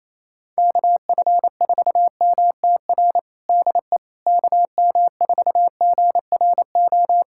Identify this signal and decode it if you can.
Morse code, heard as a keyed tone, about 28 wpm, 700 Hz. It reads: KF4MTR DE KM4GRO
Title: USA Callsign DE Callsign 28wpm